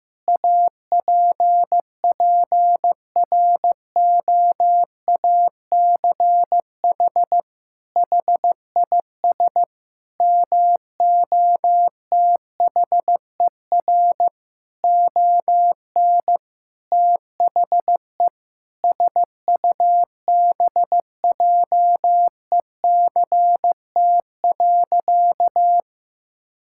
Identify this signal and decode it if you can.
Morse code, heard as a keyed tone, about 15 wpm, 700 Hz. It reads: APPROACH HIS MOTHER ON THE SUBJECT.
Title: Sentences from Top 1000 Words - Review Part 6 - 15wpm